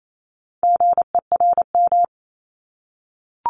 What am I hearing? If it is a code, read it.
Morse code, heard as a keyed tone, about 28 wpm, 700 Hz. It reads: GERM
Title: Multi-letter Phonemes 28wpm